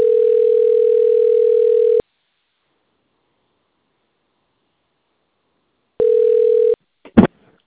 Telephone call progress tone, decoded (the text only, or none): ringback tone